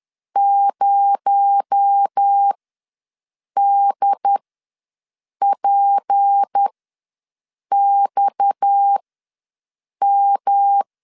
Morse code, heard as a keyed tone, about 10 wpm, spaced as for 6 wpm, 800 Hz. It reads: 0DPXM
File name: LEZIONI / 7 GF9Z / 7 GF9 ws 10wpm 4034